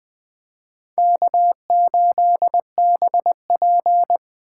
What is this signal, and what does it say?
Morse code, heard as a keyed tone, about 20 wpm, 700 Hz. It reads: K8BP
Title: USA Call Signs - 4 Characters - 3x Character Spacing - 20wpm